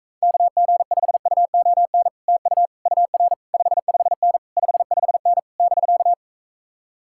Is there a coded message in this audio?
Yes: Morse code from 0.2 to 6.1 s, heard as a keyed tone, about 44 wpm, 700 Hz. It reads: KG5UON TU UR 55N 55N <BK>